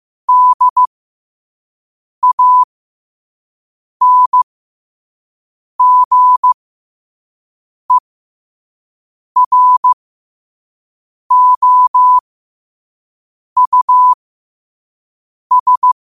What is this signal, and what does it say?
Morse code, heard as a keyed tone, about 15 wpm, spaced as for 5 wpm, 1000 Hz. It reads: DANGEROUS
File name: qotdfiles / 5wpm